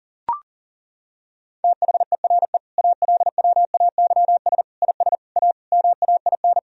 Morse code, heard as a keyed tone, about 37 wpm, 700 Hz. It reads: THERE ALWAYS IS A MAIN
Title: Sentences from Top 1000 Words - Review Part 2 - 40wpm